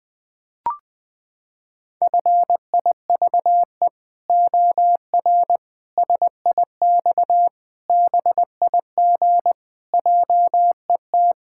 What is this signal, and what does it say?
Morse code, heard as a keyed tone, about 20 wpm, 700 Hz. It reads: FIVE OR SIX BIG JET
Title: Pangram Challenge v2 - 20wpm